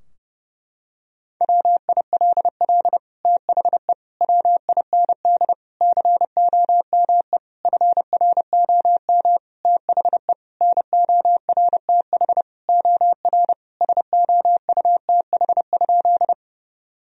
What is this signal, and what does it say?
Morse code, heard as a keyed tone, about 30 wpm, 700 Hz. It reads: WILL THE WIND COME FROM THE NORTH OR SOUTH?
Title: Sentences from Top 1000 Words - Review Part 2 - 30wpm